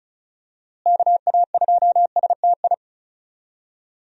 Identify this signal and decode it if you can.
Morse code, heard as a keyed tone, about 34 wpm, 700 Hz. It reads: KA2STI
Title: USA Call Signs - 6 Characters - Repeated 2x - 35wpm